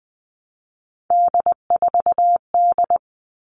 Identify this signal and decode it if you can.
Morse code, heard as a keyed tone, about 20 wpm, 700 Hz. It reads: D4D
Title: All Cut Numbers - 3-Digit 20wpm